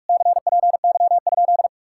Morse code, heard as a keyed tone, about 57 wpm, 700 Hz. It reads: CPY?